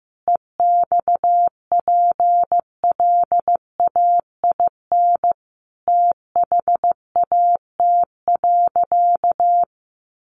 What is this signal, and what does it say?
Morse code, heard as a keyed tone, about 15 wpm, 700 Hz. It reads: EXPLAIN THAT.
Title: Sentences from Top 1100 Words 15wpm